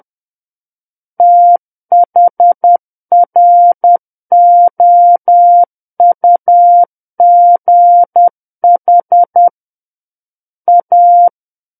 Morse code, heard as a keyed tone, about 10 wpm, 700 Hz. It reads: THROUGH A
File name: Book One / 01-Chapter 1 - 10 WPM